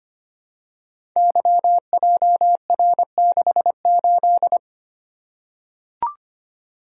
Morse code, heard as a keyed tone, about 25 wpm, 700 Hz. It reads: YJR68
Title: Code Groups - 5 Letter:Numbers - 25wpm